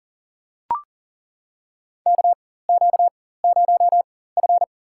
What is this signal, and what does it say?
Morse code, heard as a keyed tone, about 38 wpm, spaced as for 18 wpm, 700 Hz. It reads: KQ0F
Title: USA Call Signs - 4 Characters - 4x Character Spacing - 40wpm